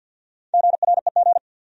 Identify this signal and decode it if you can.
Morse code, heard as a keyed tone, about 53 wpm, 700 Hz. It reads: GREG
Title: USA Common Names - 50wpm